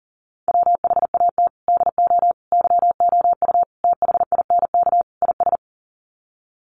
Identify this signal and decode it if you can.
Morse code, heard as a keyed tone, about 40 wpm, 700 Hz. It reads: WHAT DO YOU THINK IS
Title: Sets of 5 Words Encore 40wpm